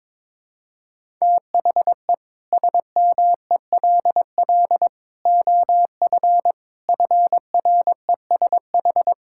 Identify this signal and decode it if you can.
Morse code, heard as a keyed tone, about 22 wpm, 700 Hz. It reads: THE SMELL OF FRESH
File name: Difficult Sentences with Hints - 22wpm